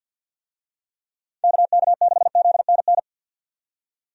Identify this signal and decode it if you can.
Morse code, heard as a keyed tone, about 48 wpm, 700 Hz. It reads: KK6ZNN
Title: USA Call Signs - 6 Characters - 50wpm